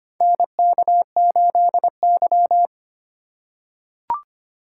Morse code, heard as a keyed tone, about 25 wpm, 700 Hz. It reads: NK8Y
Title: USA Call Signs - 4 Characters - 2x Character Spacing - 25wpm